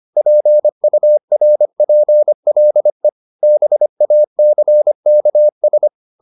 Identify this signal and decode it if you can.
Morse code, heard as a keyed tone, about 25 wpm, 600 Hz. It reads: PURPLE BACKS